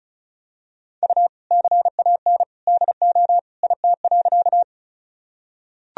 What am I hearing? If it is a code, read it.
Morse code, heard as a keyed tone, about 34 wpm, 700 Hz. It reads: U CAN DO IT.
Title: Sentences from Top 100 Words and Abbreviations 35wpm